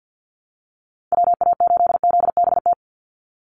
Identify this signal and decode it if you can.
Morse code, heard as a keyed tone, about 49 wpm, 700 Hz. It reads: WA8ZBT
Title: Call Signs Repeated 3x - 50wpm